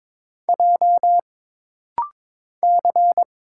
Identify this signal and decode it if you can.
Morse code, heard as a keyed tone, about 22 wpm, 700 Hz. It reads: J C
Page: This rapid-fire timing is geared to push your ICR (Instant Character Recognition) skills.